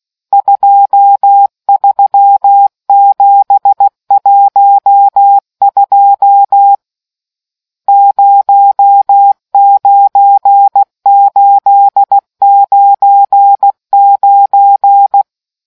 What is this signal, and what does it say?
Morse code, heard as a keyed tone, about 16 wpm, 800 Hz. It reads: 23712 09899